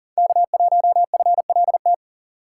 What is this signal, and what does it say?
Morse code, heard as a keyed tone, about 40 wpm, 700 Hz. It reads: K1FLT
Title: USA Call Signs - 5 Characters - 6x Character Spacing - 40wpm